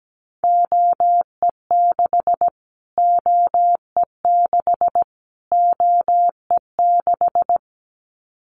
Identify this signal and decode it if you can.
Morse code, heard as a keyed tone, about 17 wpm, 700 Hz. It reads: OE6 OE6 OE6
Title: Callsign Prefixes for Top 25 DXCC - Repeated 3x - 17wpm